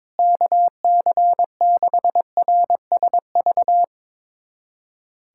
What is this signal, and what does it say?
Morse code, heard as a keyed tone, about 22 wpm, 700 Hz. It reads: KC6RSV
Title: USA Callsign DE Callsign 22wpm